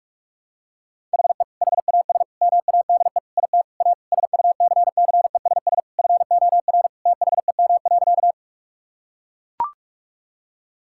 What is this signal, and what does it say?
Morse code, heard as a keyed tone, about 42 wpm, 700 Hz. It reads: HE HAS MADE IT A SUCCESS FOR THEM.
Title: Sentences from Top 1000 Words - Review Part 6 - 45wpm